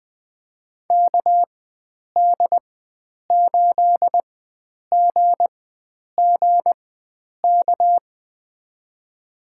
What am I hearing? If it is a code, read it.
Morse code, heard as a keyed tone, about 20 wpm, spaced as for 9 wpm, 700 Hz. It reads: KD8GGK
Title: USA Call Signs - 6 Characters - 4x Character Spacing - 20wpm